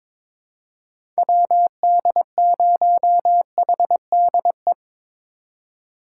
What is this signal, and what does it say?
Morse code, heard as a keyed tone, about 22 wpm, 700 Hz. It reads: WD0HDE
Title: USA Call Signs - 6 Characters - 3x Character Spacing - 22wpm